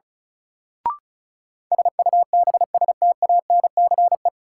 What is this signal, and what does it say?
Morse code, heard as a keyed tone, about 34 wpm, 700 Hz. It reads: SUBSTANCE
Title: Sentences from Top 1500 Words - New Word Drill - 35wpm